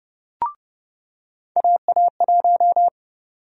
Morse code, heard as a keyed tone, about 30 wpm, 700 Hz. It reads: AA1
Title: USA Call Sign Prefixes - 30wpm